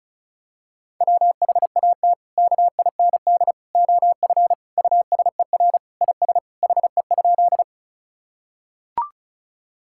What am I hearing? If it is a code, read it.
Morse code, heard as a keyed tone, about 36 wpm, 700 Hz. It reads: WHAT KIND OF USER IS HE?
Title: Sentences from Top 1000 Words - Review Part 5 - 35wpm